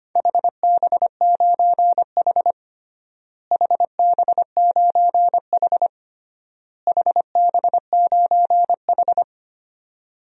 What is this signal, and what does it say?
Morse code, heard as a keyed tone, about 25 wpm, 700 Hz. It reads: HB9H HB9H HB9H
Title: DXCC-100 Call Signs - 4 Characters - Repeated 3x - 25wpm